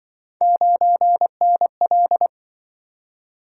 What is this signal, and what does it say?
Morse code, heard as a keyed tone, about 25 wpm, 700 Hz. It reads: 9NL
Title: Speed-Racing - Three Letter-Number Combinations - 20wpm